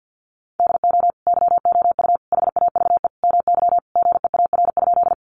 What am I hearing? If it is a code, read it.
Morse code, heard as a keyed tone, about 50 wpm, 700 Hz. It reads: DO YOU HAVE MY GEAR?
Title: Sentences from Top 1400 Words 50wpm